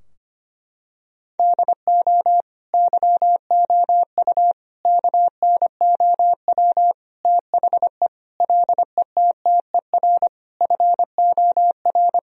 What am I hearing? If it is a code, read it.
Morse code, heard as a keyed tone, about 25 wpm, 700 Hz. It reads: DO YOU KNOW THE LETTER FOR
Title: Sentences from Top 1000 Words - Review Part 2 - 25wpm